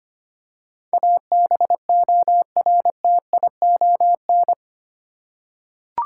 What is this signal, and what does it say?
Morse code, heard as a keyed tone, about 25 wpm, 700 Hz. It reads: ABORTION
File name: 8 Letter Words - First 3 Letters Given - 25wpm